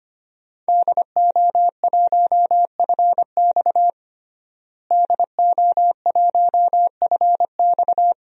Morse code, heard as a keyed tone, about 25 wpm, 700 Hz. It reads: DO1FX DO1FX
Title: DXCC-100 Call Signs - 5 Characters - Repeated 3x - 25wpm